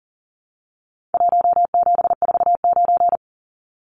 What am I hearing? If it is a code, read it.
Morse code, heard as a keyed tone, about 40 wpm, 700 Hz. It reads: 1749
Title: Numbers - Contest Style - 4-Digit 40wpm